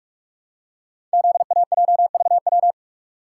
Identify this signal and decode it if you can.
Morse code, heard as a keyed tone, about 44 wpm, 700 Hz. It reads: ZAJVW
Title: Code Groups - 5 Letters - 45wpm